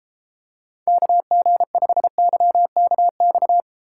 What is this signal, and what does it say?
Morse code, heard as a keyed tone, about 33 wpm, 700 Hz. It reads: KG5YKX